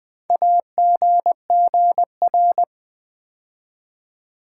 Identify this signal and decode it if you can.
Morse code, heard as a keyed tone, about 20 wpm, 700 Hz. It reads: AGGR
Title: Build A Word - 4 Letters At A Time - 10-Letter Words - 20wpm